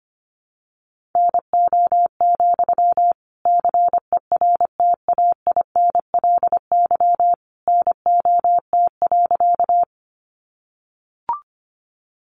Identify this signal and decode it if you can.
Morse code, heard as a keyed tone, about 25 wpm, 700 Hz. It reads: NO, CERTAINLY NOT.
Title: Sentences from Top 900 Words 25wpm